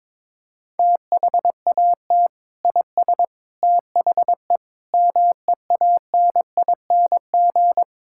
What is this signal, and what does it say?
Morse code, heard as a keyed tone, about 22 wpm, 700 Hz. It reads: THAT IS THE MEANING
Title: Sentences from Top 1000 Words - Review Part 6 - 22wpm